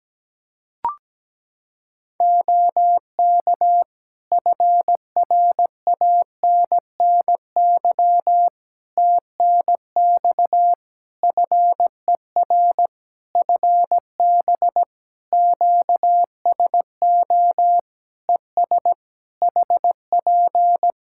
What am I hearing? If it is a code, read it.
Morse code, heard as a keyed tone, about 17 wpm, 700 Hz. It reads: OK FRANNY TNX FER FB QSO ES HP
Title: Practice QSO - Exchange 4 - Ending - 17wpm